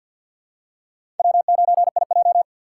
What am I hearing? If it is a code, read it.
Morse code, heard as a keyed tone, about 48 wpm, 700 Hz. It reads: W9IJ
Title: USA Call Signs - 2x Character Spacing - 50wpm